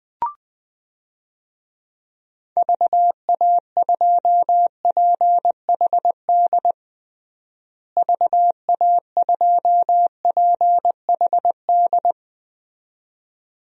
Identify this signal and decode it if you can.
Morse code, heard as a keyed tone, about 20 wpm, 700 Hz. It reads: VA2PHD VA2PHD
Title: DXCC-100 Call Signs - 6 Characters - Repeated 2x - 20wpm